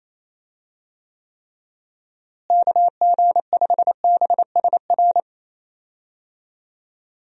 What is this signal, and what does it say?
Morse code, heard as a keyed tone, about 27 wpm, 700 Hz. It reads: KG5BSR
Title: USA Call Signs - 6 Characters - Repeated 3x - 28wpm